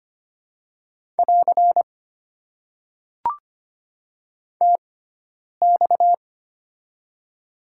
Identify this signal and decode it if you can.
Morse code, heard as a keyed tone, about 25 wpm, spaced as for 9 wpm, 700 Hz. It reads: <AR> TX
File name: Top 90 QSO Elements - 6x Character Spacing - 25wpm